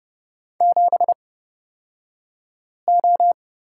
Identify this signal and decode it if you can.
Morse code, heard as a keyed tone, about 30 wpm, spaced as for 5 wpm, 700 Hz. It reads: 7O
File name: course / 30:10wpm / 235 Lesson 38d - Letter q - Two Characters 30wpm